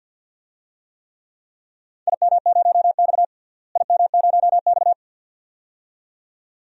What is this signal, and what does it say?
Morse code, heard as a keyed tone, about 49 wpm, 700 Hz. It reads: IM0X IM0X
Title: DXCC-100 Call Signs - 4 Characters - Repeated 2x - 50wpm